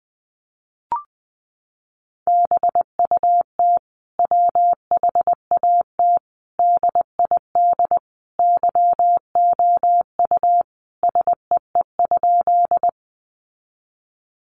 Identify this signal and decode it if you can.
Morse code, heard as a keyed tone, about 20 wpm, 700 Hz. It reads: BUT WHAT DID YOU SEE?
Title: Sentences from Top 100 Words 20wpm